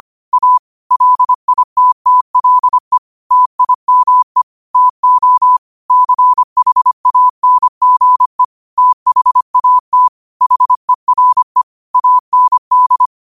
Morse code, heard as a keyed tone, about 25 wpm, 1000 Hz. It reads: A LITTLE TIME TO CHANGE THAT HERE AND
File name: qotdfiles / 25wpm